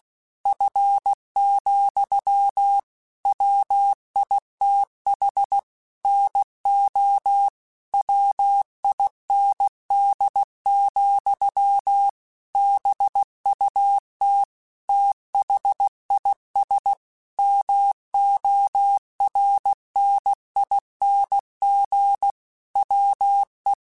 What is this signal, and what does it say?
Morse code, heard as a keyed tone, about 16 wpm, 800 Hz. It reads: F, WITH NO WIND, BUT THIS MORNING WE
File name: Weather Sets / Weather Sets-mp3s / Weather 3 17 @15